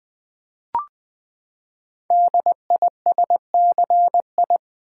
Morse code, heard as a keyed tone, about 20 wpm, 700 Hz. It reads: DISCI
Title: Build A Word - 5 Letters At A Time - 10-Letter Words - 20wpm